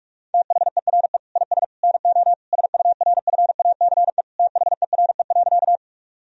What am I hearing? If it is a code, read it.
Morse code, heard as a keyed tone, about 42 wpm, 700 Hz. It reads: THERE IS NO SURFACE THERE.
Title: Sentences from Top 1000 Words - Review Part 3 - 45wpm